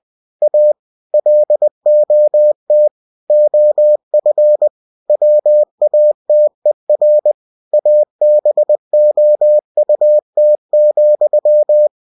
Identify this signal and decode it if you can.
Morse code, heard as a keyed tone, about 20 wpm, 600 Hz. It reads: A LOT OF WATER ABOUT,
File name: cw-books / winnie-the-pooh / winnie-20wpm-page-0096